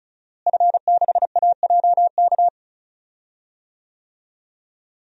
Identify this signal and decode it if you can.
Morse code, heard as a keyed tone, about 36 wpm, 700 Hz. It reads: F6AJK
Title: DXCC-100 Call Signs - 5 Characters - Repeated 3x - 35wpm